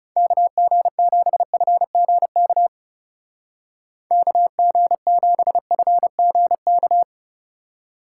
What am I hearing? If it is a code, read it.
Morse code, heard as a keyed tone, about 37 wpm, 700 Hz. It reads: KG7FGK KG7FGK
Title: Speed Racing - USA 2x3 Call Signs 30wpm